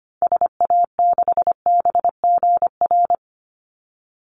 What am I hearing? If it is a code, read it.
Morse code, heard as a keyed tone, about 25 wpm, 700 Hz. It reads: SA6BGR
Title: Call Signs - 25wpm